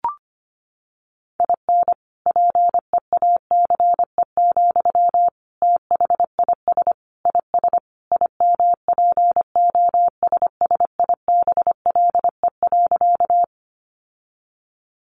Morse code, heard as a keyed tone, about 25 wpm, 700 Hz. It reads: IN PEACE, THIS IS IMPOSSIBLE.
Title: Sentences from Top 1500 Words 25wpm